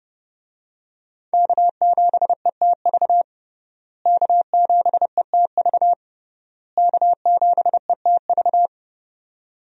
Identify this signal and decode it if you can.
Morse code, heard as a keyed tone, about 30 wpm, 700 Hz. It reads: K7ETV K7ETV K7ETV
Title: USA Call Signs - 5 Characters - Repeated 3x - 30wpm